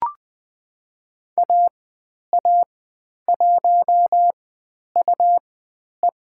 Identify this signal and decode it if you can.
Morse code, heard as a keyed tone, about 20 wpm, spaced as for 10 wpm, 700 Hz. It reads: AA1UE
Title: 109 Lesson 18g - Letter m - Call Signs 20wpm